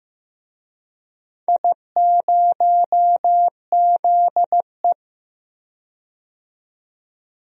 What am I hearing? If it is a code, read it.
Morse code, heard as a keyed tone, about 15 wpm, 700 Hz. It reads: I0ZE
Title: DXCC-100 Call Signs - 4 Characters - 15wpm